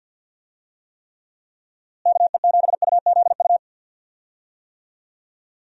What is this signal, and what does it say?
Morse code, heard as a keyed tone, about 46 wpm, 700 Hz. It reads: KE7UZU
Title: USA Call Signs - 6 Characters - Repeated 3x - 50wpm